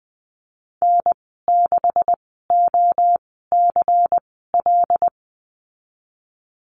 Morse code, heard as a keyed tone, about 20 wpm, spaced as for 15 wpm, 700 Hz. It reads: N6OCL
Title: 154 Lesson 25g - Letter p - Call Signs 20wpm